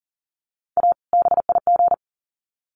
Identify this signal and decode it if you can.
Morse code, heard as a keyed tone, about 40 wpm, 700 Hz. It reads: A BIG
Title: Sets of 2 Words 40wpm